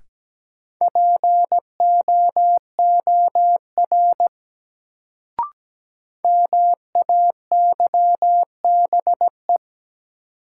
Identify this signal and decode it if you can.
Morse code, heard as a keyed tone, about 17 wpm, 700 Hz. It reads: POOR MAYBE
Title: Sentences from Top 1000 Words - New Word Drill - 17wpm